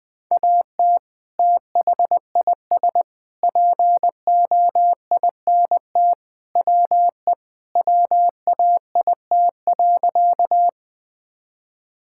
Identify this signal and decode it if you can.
Morse code, heard as a keyed tone, about 20 wpm, 700 Hz. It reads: AT THIS POINT WE WAIT.
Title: Sentences with 3-Word Phrases - 20wpm